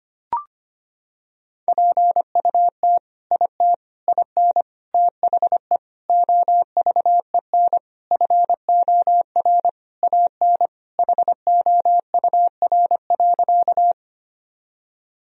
Morse code, heard as a keyed tone, about 25 wpm, 700 Hz. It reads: PUT IT IN THE OVEN FOR AN HOUR.
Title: Sentences from Top 1000 Words - Review Part 4 - 25wpm